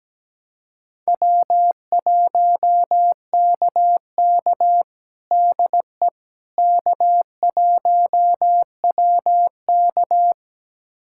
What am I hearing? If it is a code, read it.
Morse code, heard as a keyed tone, about 17 wpm, 700 Hz. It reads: W1KK DE K1WK
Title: USA Callsign DE Callsign - 4 Characters - 17wpm